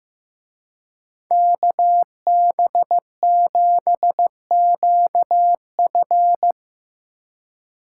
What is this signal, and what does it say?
Morse code, heard as a keyed tone, about 17 wpm, 700 Hz. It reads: KB7QF
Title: Speed Racing - USA Call Signs 15wpm